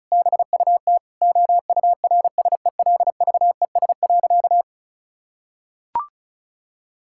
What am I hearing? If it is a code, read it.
Morse code, heard as a keyed tone, about 36 wpm, 700 Hz. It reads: BUT OURSELVES.
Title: Sentences from Top 1600 Words 35wpm